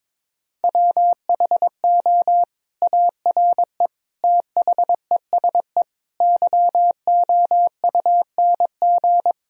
Morse code, heard as a keyed tone, about 22 wpm, 700 Hz. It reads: WHO ARE THESE YOUNG
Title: Sentences from Top 1000 Words - Review Part 1 - 22wpm